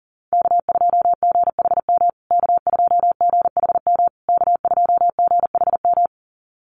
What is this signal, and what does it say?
Morse code, heard as a keyed tone, about 40 wpm, 700 Hz. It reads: K2GHM K2GHM K2GHM
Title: USA Call Signs Repeated 3x 40wpm